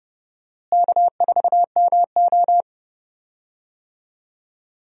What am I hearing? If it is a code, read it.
Morse code, heard as a keyed tone, about 30 wpm, 700 Hz. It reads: K4MO
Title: USA Call Signs - 4 Characters - Repeated 3x - 30wpm